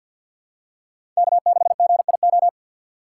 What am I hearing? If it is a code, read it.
Morse code, heard as a keyed tone, about 52 wpm, 690 Hz. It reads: K6GIO
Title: USA Call Signs - 5 Characters - 2x Character Spacing - 50wpm